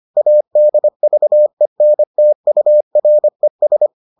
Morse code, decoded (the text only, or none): ADVENTURES